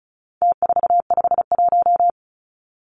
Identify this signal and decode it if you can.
Morse code, heard as a keyed tone, about 34 wpm, 700 Hz. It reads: T451
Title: Numbers - Contest Style - 4-Digit 35wpm